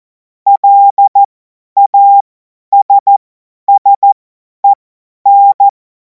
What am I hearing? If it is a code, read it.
Morse code, heard as a keyed tone, about 14 wpm, spaced as for 10 wpm, 800 Hz. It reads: LASSEN
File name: Morsekurs / Witze / Witz 14. April 2026 50 F70